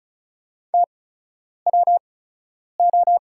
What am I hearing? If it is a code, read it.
Morse code, heard as a keyed tone, about 36 wpm, spaced as for 10 wpm, 700 Hz. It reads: TWO
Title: 8x Character Spacing - Top 100 Words - 35wpm